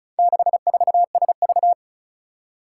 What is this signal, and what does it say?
Morse code, heard as a keyed tone, about 36 wpm, 700 Hz. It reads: 64SV
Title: Four Challenging Letter-Number Combinations 35wpm